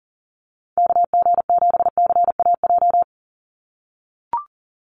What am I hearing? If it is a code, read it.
Morse code, heard as a keyed tone, about 38 wpm, 700 Hz. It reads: KG7CAJ
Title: USA Call Signs 40wpm